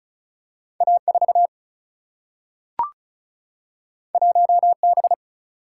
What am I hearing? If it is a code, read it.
Morse code, heard as a keyed tone, about 36 wpm, 700 Hz. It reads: A 4 1 B
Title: Single-Letter Single-Number 35wpm